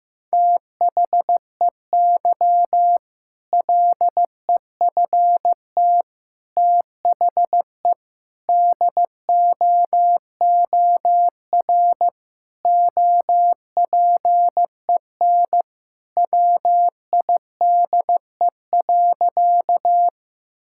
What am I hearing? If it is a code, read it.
Morse code, heard as a keyed tone, about 16 wpm, 700 Hz. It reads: THEY LEFT THE DOOR OPEN WIDE.
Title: Sentences from Top 1000 Words - Review Part 6 - 15wpm